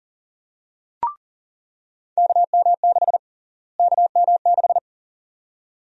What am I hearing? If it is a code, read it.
Morse code, heard as a keyed tone, about 40 wpm, 700 Hz. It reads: KM6 KM6
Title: USA Call Sign Prefixes - Repeated 2x - 40wpm